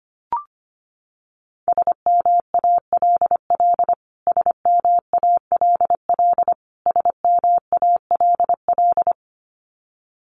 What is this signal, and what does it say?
Morse code, heard as a keyed tone, about 25 wpm, 700 Hz. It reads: SMALL SMALL SMALL
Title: Top 200 Words Repeated 25wpm